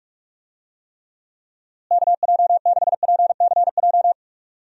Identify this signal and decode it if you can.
Morse code, heard as a keyed tone, about 46 wpm, 700 Hz. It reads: KJ6PCJ
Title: USA Call Signs - 6 Characters - Repeated 2x - 45wpm